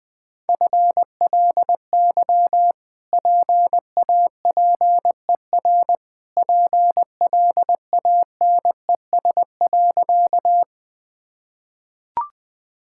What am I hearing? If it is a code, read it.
Morse code, heard as a keyed tone, about 21 wpm, 700 Hz. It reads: FLY PAPER PLANES.
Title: Easy Sentences with Hints - 20wpm